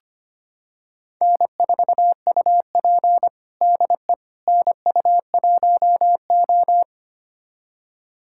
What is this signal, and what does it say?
Morse code, heard as a keyed tone, about 25 wpm, 700 Hz. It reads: N4UP DE NU1O
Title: USA Callsign DE Callsign - 4 Characters - 25wpm